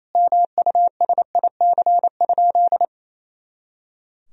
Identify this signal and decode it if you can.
Morse code, heard as a keyed tone, about 28 wpm, 700 Hz. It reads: MUSIC?
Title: Sentences from Top 300 Words and Abbreviations 28wpm